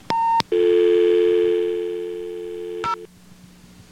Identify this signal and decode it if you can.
Dial tone (North America), 350 + 440 Hz.